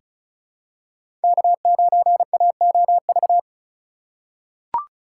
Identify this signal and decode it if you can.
Morse code, heard as a keyed tone, about 34 wpm, 700 Hz. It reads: K9AOV
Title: USA Call Signs - 5 Characters - 4x Character Spacing - 35wpm